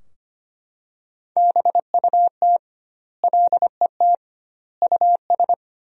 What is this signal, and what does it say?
Morse code, heard as a keyed tone, about 25 wpm, 700 Hz. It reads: BUT LET US
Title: 2x Word-Spacing - Sentences from Top 500 Words - 25wpm